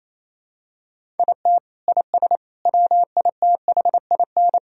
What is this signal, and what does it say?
Morse code, heard as a keyed tone, about 28 wpm, 700 Hz. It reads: IT IS WITHIN
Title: Sentences from Top 1000 Words - Review Part 4 - 28wpm